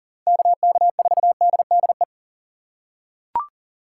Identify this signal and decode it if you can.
Morse code, heard as a keyed tone, about 38 wpm, 700 Hz. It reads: KK4DDE
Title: USA Call Signs - 6 Characters - 6x Character Spacing - 40wpm